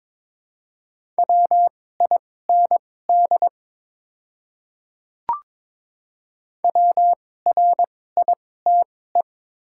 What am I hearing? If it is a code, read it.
Morse code, heard as a keyed tone, about 22 wpm, spaced as for 16 wpm, 700 Hz. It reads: WIND WRITE
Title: 2x Character Spacing - Top 500 Words - 22wpm